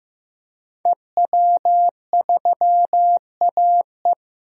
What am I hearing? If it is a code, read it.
Morse code, heard as a keyed tone, about 15 wpm, 700 Hz. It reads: EW3AE
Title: DXCC-100 Call Signs - 5 Characters - Repeated 2x - 15wpm